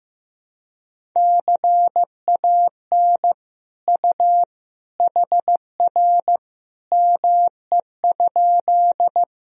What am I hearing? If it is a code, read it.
Morse code, heard as a keyed tone, about 15 wpm, 700 Hz. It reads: CAN U HR ME?
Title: Sentences from Top 300 Words and Abbreviations 15wpm